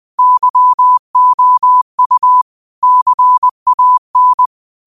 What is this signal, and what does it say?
Morse code, heard as a keyed tone, about 20 wpm, 1000 Hz. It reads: YOU CAN
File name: qotdfiles / 20wpm